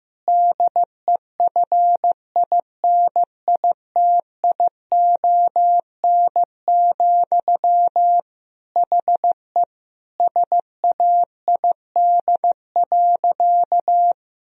Morse code, heard as a keyed tone, about 15 wpm, 700 Hz. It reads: DEFINITION, HE SAID.
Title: Sentences from Top 1000 Words - Review Part 4 - 15wpm